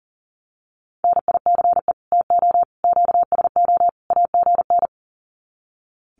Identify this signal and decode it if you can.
Morse code, heard as a keyed tone, about 40 wpm, 700 Hz. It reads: NICE TO QSO AGN
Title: Common QSO Phrases 40wpm